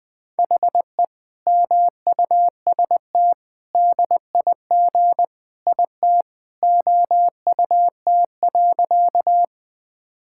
Morse code, heard as a keyed tone, about 20 wpm, 700 Hz. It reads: HE MUST DIG IT OUT.
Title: Sentences from Top 1600 Words 20wpm